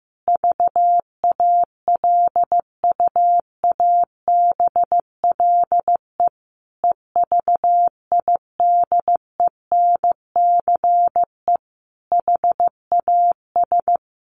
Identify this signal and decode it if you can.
Morse code, heard as a keyed tone, about 14 wpm, 700 Hz. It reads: VALUABLE EVIDENCE HAS
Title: Sentences from Top 1300 Words 15wpm